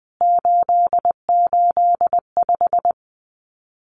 Morse code, heard as a keyed tone, about 20 wpm, 700 Hz. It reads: 885
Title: CWT Exchange 20wpm